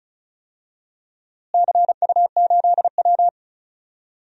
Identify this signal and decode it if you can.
Morse code, heard as a keyed tone, about 36 wpm, 700 Hz. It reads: CU8W
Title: DXCC-100 Call Signs - 4 Characters - Repeated 3x - 35wpm